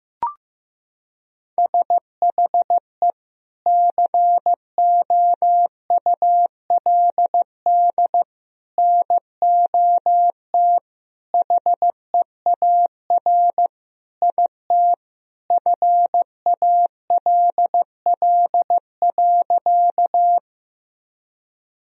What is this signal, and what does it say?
Morse code, heard as a keyed tone, about 16 wpm, 700 Hz. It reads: SHE COULD NOT HEAR IT FALL.
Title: Sentences from Top 1000 Words - Review Part 2 - 15wpm